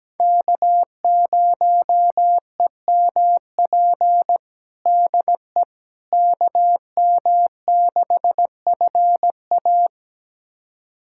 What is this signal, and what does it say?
Morse code, heard as a keyed tone, about 17 wpm, 700 Hz. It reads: K0EMP DE KM6FA
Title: USA Callsign DE Callsign - 5 Characters - 17wpm